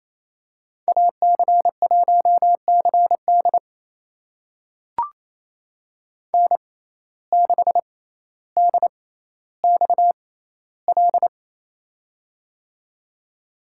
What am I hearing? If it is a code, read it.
Morse code, heard as a keyed tone, about 28 wpm, spaced as for 14 wpm, 700 Hz. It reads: AC1CD N6DXL
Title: USA Call Signs - 6x Character Spacing - 28wpm